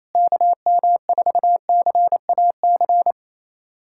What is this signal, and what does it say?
Morse code, heard as a keyed tone, about 28 wpm, 700 Hz. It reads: KM4CAC
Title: USA Callsign DE Callsign 28wpm